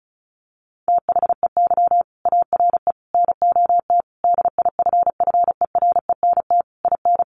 Morse code, heard as a keyed tone, about 34 wpm, 700 Hz. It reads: THEY ARE NOT DIFFERENT IN
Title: Sentences from Top 1400 Words 35wpm